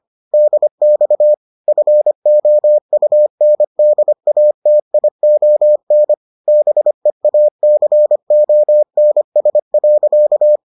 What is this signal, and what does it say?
Morse code, heard as a keyed tone, about 25 wpm, 600 Hz. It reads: DX FOUNDATION BEACONS.